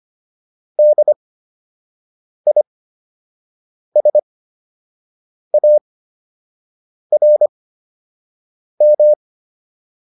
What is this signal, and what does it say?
Morse code, heard as a keyed tone, about 25 wpm, spaced as for 6 wpm, 600 Hz. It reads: DISARM